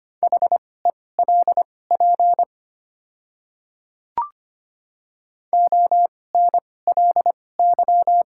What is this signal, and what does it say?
Morse code, heard as a keyed tone, about 25 wpm, spaced as for 18 wpm, 700 Hz. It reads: HELP ONLY
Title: 2x Character Spacing - Top 200 Words - 25wpm